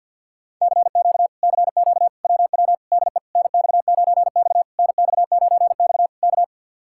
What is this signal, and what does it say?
Morse code, heard as a keyed tone, about 67 wpm, spaced as for 58 wpm, 700 Hz. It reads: CQ CQ WW DE NX9X NX9X K
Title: Speed-Racing Common QSO Phrases - 50wpm